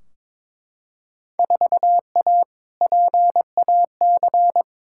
Morse code, heard as a keyed tone, about 21 wpm, 700 Hz. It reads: 4A PAC
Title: Field Day Exchanges Encore - 22wpm